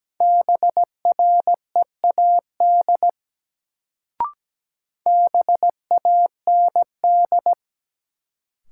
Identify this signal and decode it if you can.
Morse code, heard as a keyed tone, about 17 wpm, 700 Hz. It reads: BREAD BAND
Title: Sentences from Top 1600 Words - New Word Drill - 17wpm